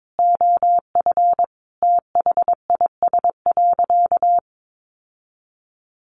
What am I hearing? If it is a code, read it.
Morse code, heard as a keyed tone, about 22 wpm, 700 Hz. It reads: OF THIS.
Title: Sentences from Top 1500 Words 22wpm